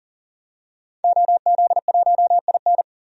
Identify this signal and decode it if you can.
Morse code, heard as a keyed tone, about 37 wpm, 700 Hz. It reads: OZ1IN
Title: DXCC-100 Call Signs - 5 Characters - 40wpm